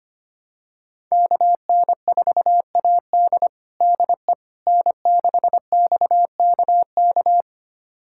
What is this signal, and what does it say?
Morse code, heard as a keyed tone, about 25 wpm, 700 Hz. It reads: KN4AD DE N6XKK
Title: USA Callsign DE Callsign - 5 Characters - 25wpm